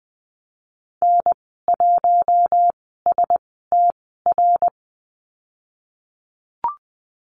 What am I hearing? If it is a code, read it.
Morse code, heard as a keyed tone, about 20 wpm, spaced as for 15 wpm, 700 Hz. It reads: N1STR
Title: 055 Lesson 10g - Letter r - Call Signs 20wpm